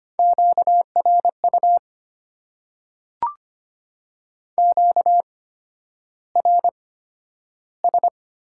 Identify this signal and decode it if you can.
Morse code, heard as a keyed tone, about 25 wpm, spaced as for 11 wpm, 700 Hz. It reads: QRU QRS
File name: Top 90 QSO Elements - 8x Character Spacing - 25wpm